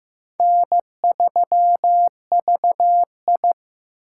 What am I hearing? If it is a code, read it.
Morse code, heard as a keyed tone, about 15 wpm, 700 Hz. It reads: N3VI
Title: USA Call Signs - 4 Characters - 8x Character Spacing - 15wpm